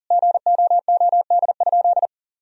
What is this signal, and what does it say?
Morse code, heard as a keyed tone, about 40 wpm, 700 Hz. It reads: GOOD?
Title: Sentences from Top 1000 Words - Review Part 3 - 40wpm